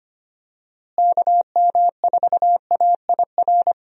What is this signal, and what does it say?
Morse code, heard as a keyed tone, about 25 wpm, 700 Hz. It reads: KM4AIR
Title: USA Call Signs - 6 Characters - Repeated 2x - 25wpm